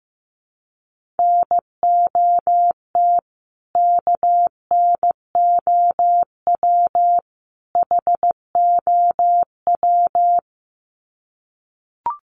Morse code, heard as a keyed tone, about 15 wpm, 700 Hz. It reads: NOT KNOW HOW
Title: Sets of 3 Words 15wpm